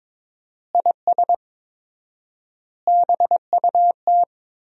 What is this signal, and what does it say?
Morse code, heard as a keyed tone, about 22 wpm, 700 Hz. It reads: IS BUT
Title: 4x Word-Spacing - Sentences from Top 100 Words - 22wpm